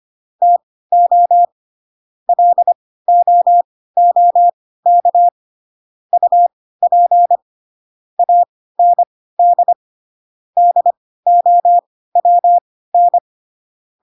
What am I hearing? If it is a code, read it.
Morse code, heard as a keyed tone, about 25 wpm, spaced as for 16 wpm, 700 Hz. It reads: TO LOOK UP AND DOWN
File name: Pinocchio / Pinocchio ch1 10 25 ik1hge